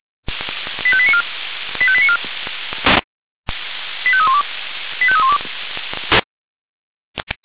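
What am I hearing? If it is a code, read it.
Five-tone selective call (ZVEI), heard as quick runs of five tones, tones of about 70 ms. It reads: 85894 85894 85312 85312
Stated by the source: ZVEI five-tone selective call received on an SDR.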